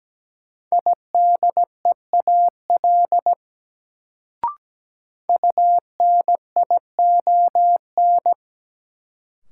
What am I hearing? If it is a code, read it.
Morse code, heard as a keyed tone, about 17 wpm, 700 Hz. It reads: IDEAL UNION
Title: Sentences from Top 1400 Words - New Word Drill - 17wpm